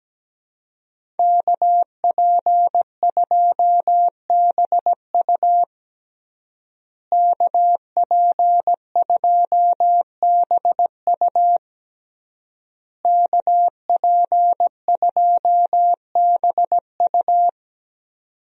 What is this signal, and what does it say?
Morse code, heard as a keyed tone, about 17 wpm, 700 Hz. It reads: KP2BU KP2BU KP2BU
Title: USA Call Signs - 5 Characters - Repeated 3x - 17wpm